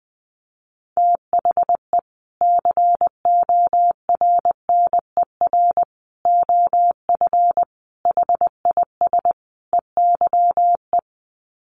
Morse code, heard as a keyed tone, about 19 wpm, 700 Hz. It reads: THE CORNER OF HIS EYE
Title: Sets of 5 Words Encore2 20wpm